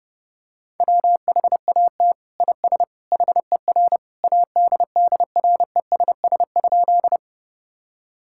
Morse code, heard as a keyed tone, about 32 wpm, 700 Hz. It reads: WHAT IS HER ADDRESS?
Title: Sentences from Top 1000 Words - Review Part 6 - 30wpm